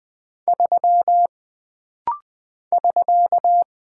Morse code, heard as a keyed tone, about 20 wpm, 700 Hz. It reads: 3 <SK>
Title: Single Character (Letter:Number:Punctuation:Prosign) - Rapid-Fire - 20wpm